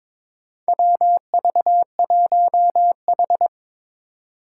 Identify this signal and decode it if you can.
Morse code, heard as a keyed tone, about 22 wpm, 700 Hz. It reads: WV1H
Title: Four Challenging Letter-Number Combinations 22wpm